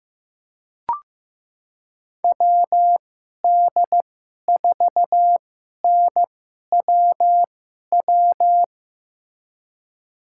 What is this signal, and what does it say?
Morse code, heard as a keyed tone, about 15 wpm, spaced as for 11 wpm, 700 Hz. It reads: WD4NWW
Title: USA Call Signs - 6 Characters - 2x Character Spacing - 15wpm